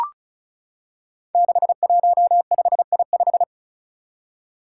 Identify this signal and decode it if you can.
Morse code, heard as a keyed tone, about 36 wpm, 700 Hz. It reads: 615I5
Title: Five Challenging Letter-Number Combinations 35wpm